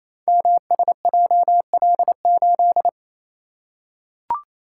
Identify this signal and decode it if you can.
Morse code, heard as a keyed tone, about 28 wpm, 700 Hz. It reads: MSJL8
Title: Code Groups - 5 Letter:Numbers - 28wpm